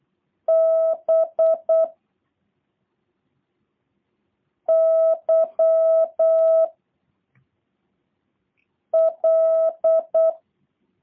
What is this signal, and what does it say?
Morse code, heard as a keyed tone, about 8 wpm, 650 Hz. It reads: B Y L
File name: jh8cbh / 2023cw / review1-6